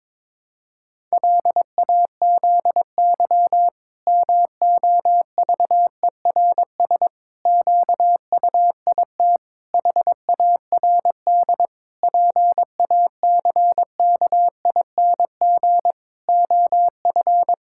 Morse code, heard as a keyed tone, about 22 wpm, 700 Hz. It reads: LAZY MOVERS QUIT HARD PACKING OF
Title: Pangram Challenge v2 - 22wpm